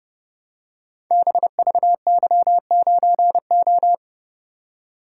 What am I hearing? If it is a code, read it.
Morse code, heard as a keyed tone, about 30 wpm, 700 Hz. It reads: BVY9O
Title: Code Groups - 5 Letter:Numbers - 30wpm